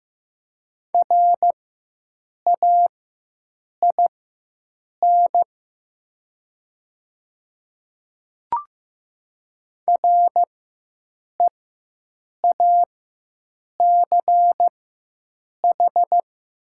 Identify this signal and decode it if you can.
Morse code, heard as a keyed tone, about 15 wpm, spaced as for 7 wpm, 700 Hz. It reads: RAIN REACH
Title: 4x Character Spacing - Top 500 Words - 15wpm